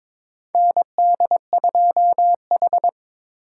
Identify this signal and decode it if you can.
Morse code, heard as a keyed tone, about 21 wpm, 700 Hz. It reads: ND2H
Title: USA Call Signs - 3x Character Spacing - 22wpm